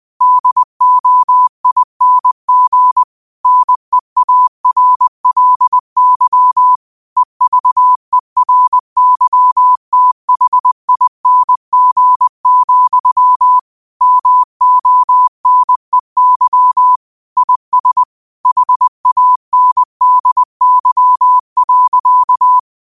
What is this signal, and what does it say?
Morse code, heard as a keyed tone, about 20 wpm, 1000 Hz. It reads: DOING NEARLY EVERYTHING, MONEY IS HANDY.